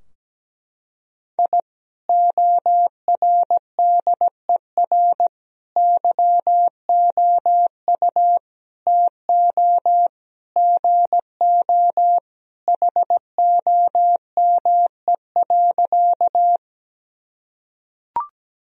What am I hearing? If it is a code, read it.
Morse code, heard as a keyed tone, about 17 wpm, 700 Hz. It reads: I ORDER YOU TO GO HOME.
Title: Sentences from Top 1000 Words - Review Part 2 - 17wpm